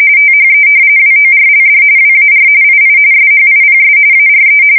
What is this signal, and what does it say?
RTTY, about 45.45 baud. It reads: THE WORLD WAR II SHIPS ARE
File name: fsk170 / battleships